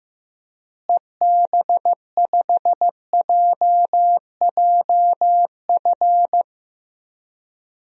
Morse code, heard as a keyed tone, about 15 wpm, 700 Hz. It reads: EB5JJF
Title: DXCC-100 Call Signs - 6 Characters - Repeated 2x - 15wpm